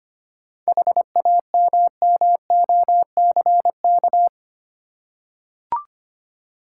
Morse code, heard as a keyed tone, about 25 wpm, 690 Hz. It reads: HAMMOCK
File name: practice / 25wpm / Easy Words with Hints - 25wpm